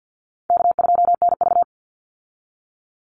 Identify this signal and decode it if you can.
Morse code, heard as a keyed tone, about 50 wpm, 700 Hz. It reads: K2NV
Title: Easy Call Signs - 50wpm